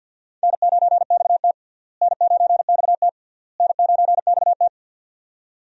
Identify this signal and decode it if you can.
Morse code, heard as a keyed tone, about 46 wpm, 700 Hz. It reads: N9XT N9XT N9XT
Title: USA Call Signs - 4 Characters - Repeated 3x - 50wpm